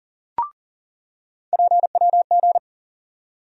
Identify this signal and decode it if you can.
Morse code, heard as a keyed tone, about 40 wpm, 700 Hz. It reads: PWG
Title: USA Call Sign Postfixes - 3 Letters 40wpm